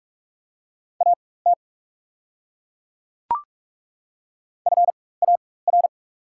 Morse code, heard as a keyed tone, about 42 wpm, spaced as for 21 wpm, 690 Hz. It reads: AT FAR
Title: 4x Character Spacing - Top 500 Words - 45wpm